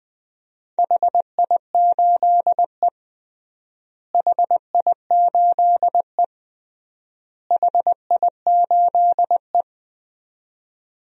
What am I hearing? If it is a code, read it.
Morse code, heard as a keyed tone, about 21 wpm, 700 Hz. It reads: HI8E HI8E HI8E
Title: DXCC-100 Call Signs - 4 Characters - Repeated 3x - 20wpm